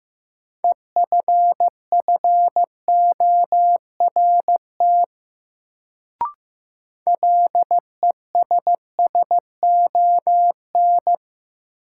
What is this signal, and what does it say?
Morse code, heard as a keyed tone, about 15 wpm, 700 Hz. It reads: EFFORT LESSON
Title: Sentences from Top 1200 Words - New Word Drill - 15wpm